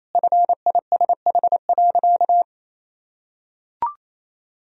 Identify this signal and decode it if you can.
Morse code, heard as a keyed tone, about 28 wpm, 700 Hz. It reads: FISH.
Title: Sentences from Top 1000 Words - Review Part 1 - 28wpm